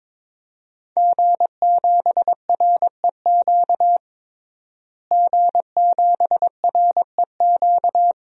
Morse code, heard as a keyed tone, about 22 wpm, 700 Hz. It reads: G7REQ G7REQ
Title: DXCC-100 Call Signs - 5 Characters - Repeated 3x - 22wpm